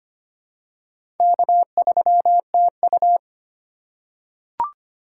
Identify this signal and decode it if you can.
Morse code, heard as a keyed tone, about 25 wpm, 700 Hz. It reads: K3TU
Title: USA Call Signs - 4 Characters - 3x Character Spacing - 25wpm